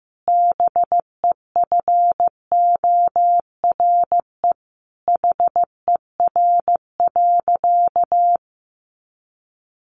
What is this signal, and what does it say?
Morse code, heard as a keyed tone, about 15 wpm, 700 Hz. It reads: BEFORE HER.
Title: Sentences from Top 1400 Words 15wpm